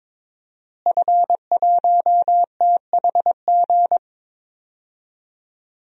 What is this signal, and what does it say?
Morse code, heard as a keyed tone, about 22 wpm, 700 Hz. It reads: F1THG
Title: DXCC-100 Call Signs - 5 Characters - Repeated 3x - 22wpm